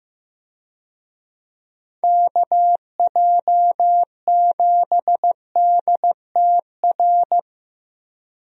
Morse code, heard as a keyed tone, about 15 wpm, 700 Hz. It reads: KJ7DTR
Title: USA Call Signs - 6 Characters - Repeated 3x - 15wpm